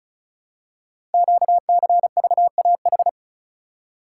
Morse code, heard as a keyed tone, about 36 wpm, 700 Hz. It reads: QCVAH
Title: Code Groups - 5 Letters - 35wpm